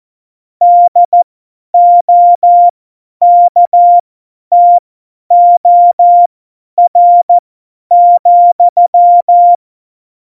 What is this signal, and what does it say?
Morse code, heard as a keyed tone, about 14 wpm, spaced as for 10 wpm, 700 Hz. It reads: DOKTOR,
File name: Morsekurs / Witze / Witz 22. September 2020, 50 F70